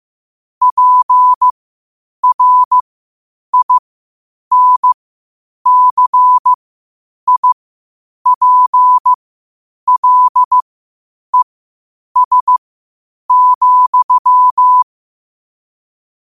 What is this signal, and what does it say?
Morse code, heard as a keyed tone, about 15 wpm, spaced as for 8 wpm, 1000 Hz. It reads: PRINCIPLES,
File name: qotdfiles / 8wpm